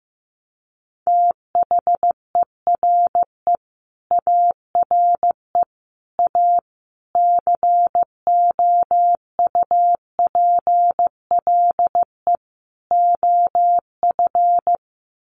Morse code, heard as a keyed tone, about 15 wpm, 700 Hz. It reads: THERE ARE A COUPLE OF
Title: Sets of 5 Words Encore 15wpm